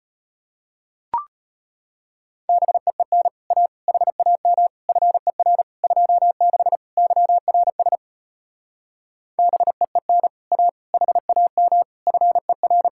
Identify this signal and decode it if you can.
Morse code, heard as a keyed tone, about 33 wpm, 700 Hz. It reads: BEEN A HAM FER 26 YRS BEEN A HAM FER